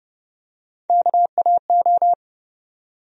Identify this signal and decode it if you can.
Morse code, heard as a keyed tone, about 32 wpm, 700 Hz. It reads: KAO
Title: USA Call Sign Postfixes - 3 Letters 30wpm